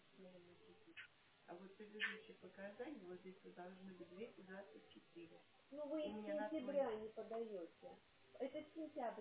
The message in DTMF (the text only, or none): none